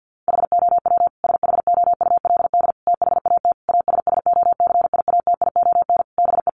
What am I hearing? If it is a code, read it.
Morse code, heard as a keyed tone, about 50 wpm, 700 Hz. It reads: HOW SHOULD THAT ASSOCIATION BE
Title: Sentences from Top 1300 Words 50wpm